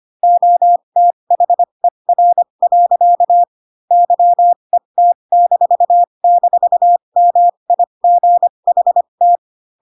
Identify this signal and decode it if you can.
Morse code, heard as a keyed tone, about 25 wpm, spaced as for 22 wpm, 700 Hz. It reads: OTHER. YET--MIGHT